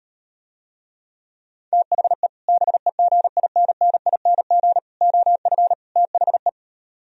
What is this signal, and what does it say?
Morse code, heard as a keyed tone, about 39 wpm, 690 Hz. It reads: THE BEGINNING OF THE